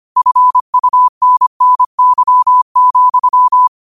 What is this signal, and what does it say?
Morse code, heard as a keyed tone, about 25 wpm, 1000 Hz. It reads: FUNNY,
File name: qotdfiles / 25wpm